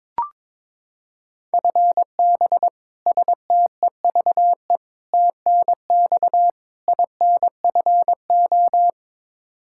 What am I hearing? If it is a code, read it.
Morse code, heard as a keyed tone, about 22 wpm, 700 Hz. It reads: FB STEVE TNX INFO
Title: Practice QSO - Exchange 3 - More Info - 22wpm